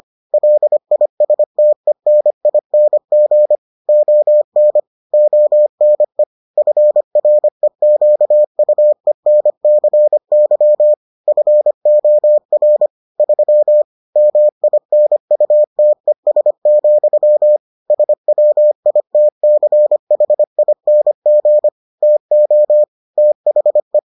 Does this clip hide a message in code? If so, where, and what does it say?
Yes: Morse code from 0.3 to 24.0 s, heard as a keyed tone, about 25 wpm, 600 Hz. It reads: LISTENING ON ONE FREQUENCY FOR 3 MINUTES, SWITCHING TO THE